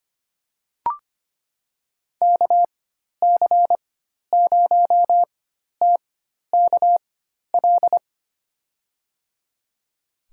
Morse code, heard as a keyed tone, about 25 wpm, spaced as for 12 wpm, 700 Hz. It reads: KC0TKL
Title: USA Call Signs - 6 Characters - 4x Character Spacing - 25wpm